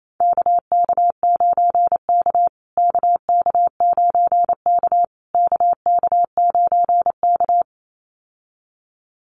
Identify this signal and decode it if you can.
Morse code, heard as a keyed tone, about 28 wpm, 700 Hz. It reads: KK9K KK9K KK9K
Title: USA Call Signs Repeated 3x 28wpm